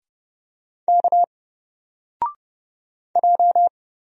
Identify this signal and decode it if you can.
Morse code, heard as a keyed tone, about 30 wpm, 700 Hz. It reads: K J
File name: Single-Letter-Number-30wpm